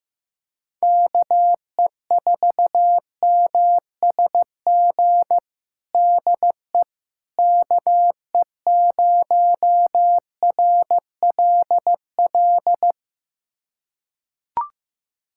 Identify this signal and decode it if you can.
Morse code, heard as a keyed tone, about 16 wpm, 700 Hz. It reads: KE4MSG DE KE0RLL
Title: USA Callsign DE Callsign - 6 Characters - 15wpm